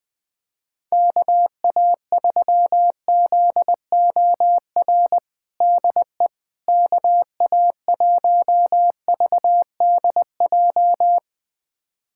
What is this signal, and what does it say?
Morse code, heard as a keyed tone, about 20 wpm, 700 Hz. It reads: KA3ZOR DE KA1VDJ
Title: USA Callsign DE Callsign - 6 Characters - 20wpm